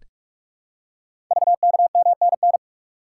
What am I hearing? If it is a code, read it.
Morse code, heard as a keyed tone, about 44 wpm, 690 Hz. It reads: VKMNN
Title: Code Groups - 5 Letters - 45wpm